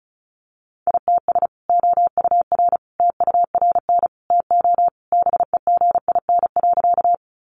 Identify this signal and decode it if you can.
Morse code, heard as a keyed tone, about 34 wpm, 700 Hz. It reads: ITS OUR TURN TO BEGIN.
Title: Sentences from Top 600 Words 35wpm